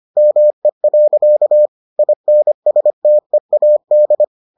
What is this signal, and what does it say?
Morse code, heard as a keyed tone, about 25 wpm, 600 Hz. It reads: ME. INSTEAD